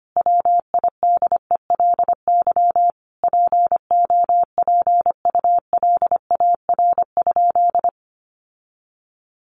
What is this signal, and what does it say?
Morse code, heard as a keyed tone, about 25 wpm, 700 Hz. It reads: WIDELY POPULAR?
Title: Sentences from Top 1200 Words 25wpm